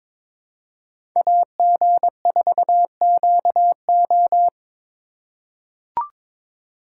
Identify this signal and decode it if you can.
Morse code, heard as a keyed tone, about 22 wpm, 700 Hz. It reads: AG4QO
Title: USA Call Signs - 5 Characters - 4x Character Spacing - 22wpm